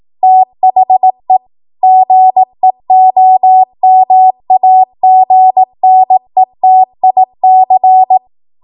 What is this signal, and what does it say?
Morse code, heard as a keyed tone, about 18 wpm, 750 Hz. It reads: THE GEOMAGNETIC
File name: Archive / 18 WPM / 18WPM